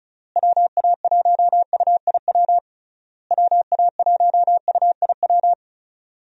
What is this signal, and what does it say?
Morse code, heard as a keyed tone, about 36 wpm, 700 Hz. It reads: WA1UIW WA1UIW
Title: USA Call Signs - 6 Characters - Repeated 3x - 35wpm